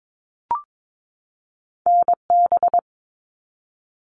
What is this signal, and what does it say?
Morse code, heard as a keyed tone, about 22 wpm, 700 Hz. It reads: NB
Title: Canadian Provinces and Territories 22wpm